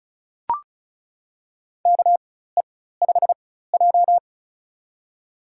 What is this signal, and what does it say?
Morse code, heard as a keyed tone, about 36 wpm, spaced as for 16 wpm, 700 Hz. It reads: KE5J
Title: USA Call Signs - 4 Characters - 4x Character Spacing - 35wpm